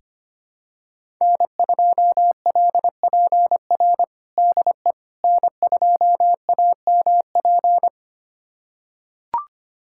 Morse code, heard as a keyed tone, about 25 wpm, 700 Hz. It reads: N2LPR DE N2AMP
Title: USA Callsign DE Callsign - 5 Characters - 25wpm